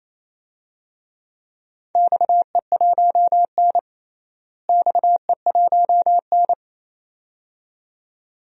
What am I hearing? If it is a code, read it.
Morse code, heard as a keyed tone, about 28 wpm, 700 Hz. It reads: XE1N XE1N
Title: DXCC-100 Call Signs - 4 Characters - Repeated 2x - 28wpm